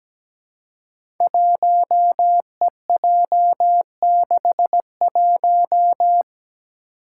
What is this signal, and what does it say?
Morse code, heard as a keyed tone, about 17 wpm, 700 Hz. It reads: 1EJ61
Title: Five Challenging Letter-Number Combinations 17wpm